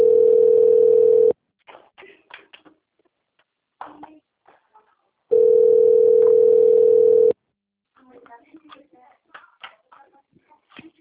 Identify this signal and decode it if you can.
Ringback tone (North America), 440 + 480 Hz, with speech.
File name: rg-1600-unknown- - - .38455